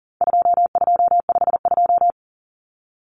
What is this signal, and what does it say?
Morse code, heard as a keyed tone, about 39 wpm, 700 Hz. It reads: 2252
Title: Common Cut Numbers - 4-Digit 40wpm